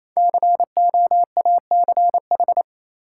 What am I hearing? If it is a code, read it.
Morse code, heard as a keyed tone, about 28 wpm, 700 Hz. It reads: COACH
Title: Five Letter Words - 28wpm